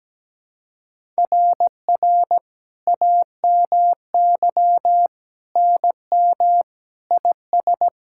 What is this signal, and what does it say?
Morse code, heard as a keyed tone, about 17 wpm, 700 Hz. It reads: RR AMY NM IS
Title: SOTA QSO Practice 17wpm